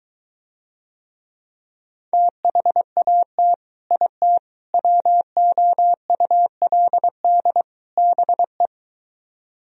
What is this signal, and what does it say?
Morse code, heard as a keyed tone, about 23 wpm, 700 Hz. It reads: THAT IT WOULD BE